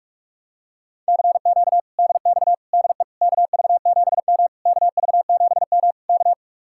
Morse code, heard as a keyed tone, about 49 wpm, 700 Hz. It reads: CQ DX DE KV7M KV7M K